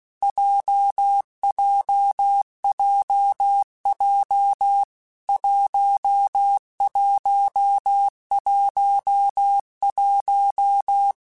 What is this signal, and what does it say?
Morse code, heard as a keyed tone, about 16 wpm, 800 Hz. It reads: JJJJ 1111
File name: Basic Letter Groups By7Groups / JZ17 / JZ17@15